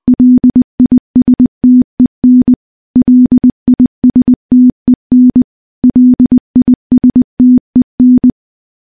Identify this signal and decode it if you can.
Morse code, heard as a keyed tone, about 21 wpm, 250 Hz. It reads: LISTEN LISTEN LISTEN
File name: morse-training-20WPM